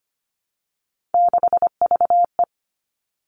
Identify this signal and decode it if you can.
Morse code, heard as a keyed tone, about 25 wpm, 700 Hz. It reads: 6VE
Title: Three Challenging Letter-Number Combinations 25wpm